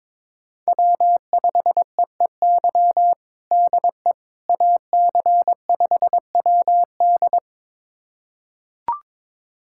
Morse code, heard as a keyed tone, about 22 wpm, 700 Hz. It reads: W5EEY DE AC5WD
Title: USA Callsign DE Callsign - 5 Characters - 22wpm